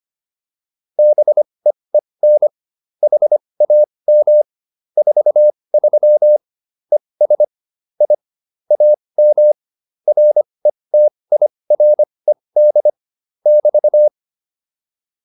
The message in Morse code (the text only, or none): BEEN HAM 43 ES I AM RETIRED <BT>